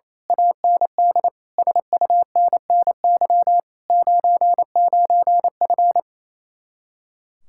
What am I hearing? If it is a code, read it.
Morse code, heard as a keyed tone, about 27 wpm, 700 Hz. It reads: AND SUNNY 99F
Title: Speed-Racing Common QSO Phrases - 28wpm